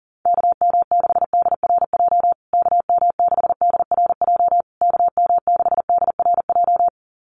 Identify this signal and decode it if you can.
Morse code, heard as a keyed tone, about 37 wpm, 700 Hz. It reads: KM6DRJ KM6DRJ KM6DRJ
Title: USA Call Signs Repeated 3x 40wpm